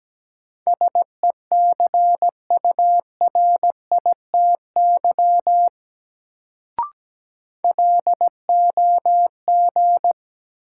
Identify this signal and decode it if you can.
Morse code, heard as a keyed tone, about 17 wpm, 700 Hz. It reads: SECURITY LOG
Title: Sentences from Top 1400 Words - New Word Drill Alt - 17wpm